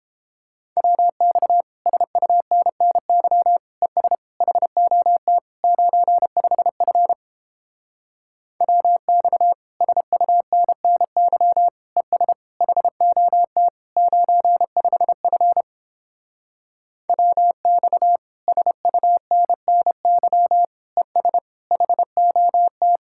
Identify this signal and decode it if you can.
Morse code, heard as a keyed tone, about 31 wpm, 700 Hz. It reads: WX SUNNY ES HOT 95F WX SUNNY ES HOT 95F WX SUNNY ES HOT